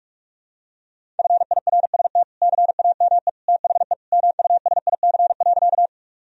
Morse code, heard as a keyed tone, about 46 wpm, 700 Hz. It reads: FIRST CAME THE MUSIC.
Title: Sentences from Top 1000 Words - Review Part 1 - 45wpm